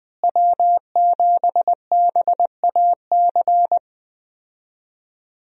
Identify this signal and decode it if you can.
Morse code, heard as a keyed tone, about 21 wpm, 700 Hz. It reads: W7BAC
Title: USA Call Signs - 5 Characters - Repeated 3x - 20wpm